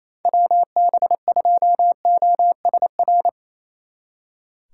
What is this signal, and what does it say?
Morse code, heard as a keyed tone, about 37 wpm, 700 Hz. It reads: WB2OSR